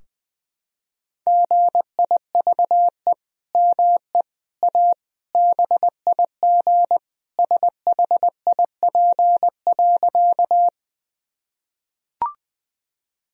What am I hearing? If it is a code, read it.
Morse code, heard as a keyed tone, about 20 wpm, 700 Hz. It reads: GIVE ME A BIG SHIP.
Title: Sentences from Top 1000 Words - Review Part 2 - 20wpm